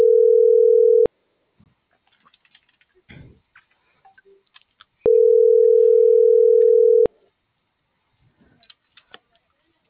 Ringback tone (North America), 440 + 480 Hz.
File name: CENTRAL TERRESTRE / out lucero.sky